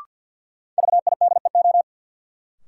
Morse code, heard as a keyed tone, about 50 wpm, 700 Hz. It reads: VIDEO
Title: Sentences from Top 1000 Words - Review Part 4 - New Word Drill - 50wpm